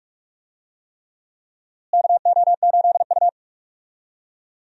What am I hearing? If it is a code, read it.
Morse code, heard as a keyed tone, about 46 wpm, 700 Hz. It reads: KO8U
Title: USA Call Signs - 4 Characters - Repeated 3x - 45wpm